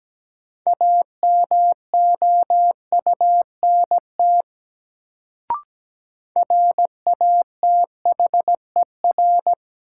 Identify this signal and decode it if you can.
Morse code, heard as a keyed tone, about 17 wpm, 700 Hz. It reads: AMOUNT RATHER
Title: Sentences from Top 700 Words - New Word Drill - 17wpm